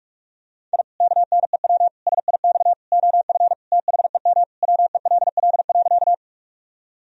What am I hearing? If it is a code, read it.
Morse code, heard as a keyed tone, about 44 wpm, 700 Hz. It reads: I KNEW SIX OF THEM WELL.
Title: Sentences from Top 1000 Words - Review Part 2 - 45wpm